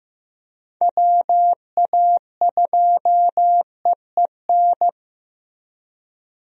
Morse code, heard as a keyed tone, about 15 wpm, 700 Hz. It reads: WA2EEN
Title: USA Call Signs - 6 Characters - Repeated 3x - 15wpm